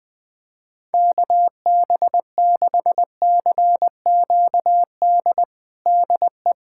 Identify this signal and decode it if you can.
Morse code, heard as a keyed tone, about 20 wpm, 700 Hz. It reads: KB6CQD DE
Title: USA Callsign DE Callsign 20wpm